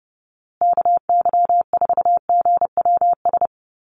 Morse code, heard as a keyed tone, about 30 wpm, 700 Hz. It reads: KY4GWS
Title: USA Call Signs Repeated 3x 30wpm